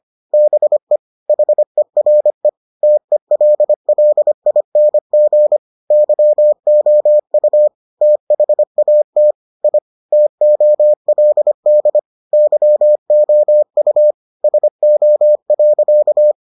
Morse code, heard as a keyed tone, about 25 wpm, 600 Hz. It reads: BE HERE TELLING YOU THAT I TOLD YOU SO.